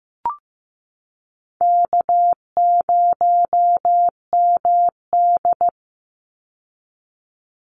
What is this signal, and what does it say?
Morse code, heard as a keyed tone, about 15 wpm, 700 Hz. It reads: K0MD
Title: Call Signs Repeated 2x - 15wpm